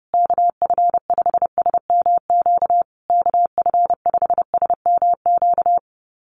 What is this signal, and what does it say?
Morse code, heard as a keyed tone, about 30 wpm, 700 Hz. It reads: KF5SMQ KF5SMQ
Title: USA Call Signs Repeated 3x 30wpm